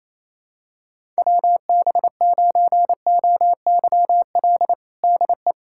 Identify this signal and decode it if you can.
Morse code, heard as a keyed tone, about 28 wpm, 700 Hz. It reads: WB9OYL DE
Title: USA Callsign DE Callsign - 6 Characters - 28wpm